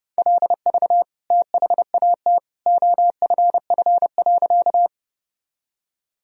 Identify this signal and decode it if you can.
Morse code, heard as a keyed tone, about 30 wpm, 700 Hz. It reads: LV THAT OFF.
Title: Sentences from Top 400 Words and Abbreviations 30wpm